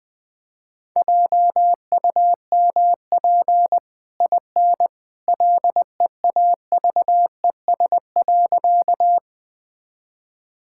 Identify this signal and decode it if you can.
Morse code, heard as a keyed tone, about 21 wpm, 700 Hz. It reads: JUMP IN LEAVES.
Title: Easy Sentences with Hints - 20wpm